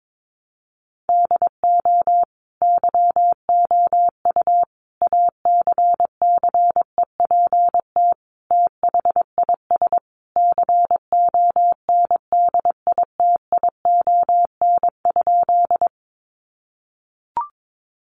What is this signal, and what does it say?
Morse code, heard as a keyed tone, about 22 wpm, 700 Hz. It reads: DO YOU ACCEPT THIS CONDITION?
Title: Sentences from Top 1300 Words 22wpm